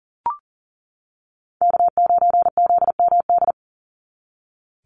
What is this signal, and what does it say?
Morse code, heard as a keyed tone, about 40 wpm, 700 Hz. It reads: K9ZMD
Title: NA SOTA Chaser Callsigns 40wpm